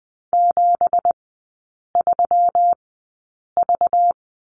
Morse code, heard as a keyed tone, about 20 wpm, spaced as for 8 wpm, 700 Hz. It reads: 73V
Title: 205 Lesson 33e - Letter j - Three Characters 20wpm